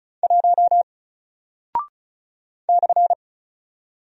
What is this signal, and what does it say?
Morse code, heard as a keyed tone, about 36 wpm, 700 Hz. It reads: 1 /
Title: Single Character (Letter:Number:Punctuation:Prosign) 35wpm